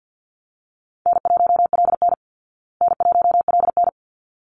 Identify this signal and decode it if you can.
Morse code, heard as a keyed tone, about 50 wpm, 700 Hz. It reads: N1LN N1LN
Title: Easy Call Signs Repeated 2x - 50wpm